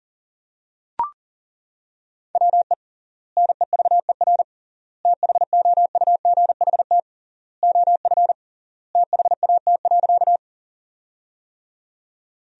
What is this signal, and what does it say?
Morse code, heard as a keyed tone, about 37 wpm, 700 Hz. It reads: WE NEVER THOUGHT OF THAT.
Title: 3x Word-Spacing - Sentences from Top 300 Words - 40wpm